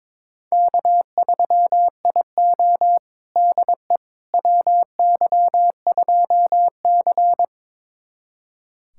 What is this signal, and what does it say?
Morse code, heard as a keyed tone, about 22 wpm, 700 Hz. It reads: K3IO DE WY2C
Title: USA Callsign DE Callsign - 4 Characters - 22wpm